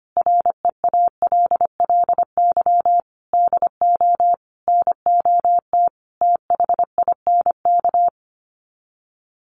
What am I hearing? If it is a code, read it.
Morse code, heard as a keyed tone, about 25 wpm, 700 Hz. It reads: REALLY DO NOT THINK
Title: Sets of 5 Words Encore2 25wpm